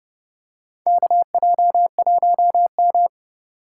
Morse code, heard as a keyed tone, about 32 wpm, 700 Hz. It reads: KJ1M
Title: USA Call Signs - 4 Characters - 3x Character Spacing - 30wpm